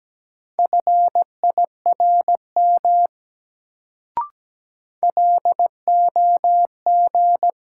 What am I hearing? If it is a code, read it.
Morse code, heard as a keyed tone, about 17 wpm, 700 Hz. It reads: FIRM LOG
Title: Sentences from Top 1400 Words - New Word Drill - 17wpm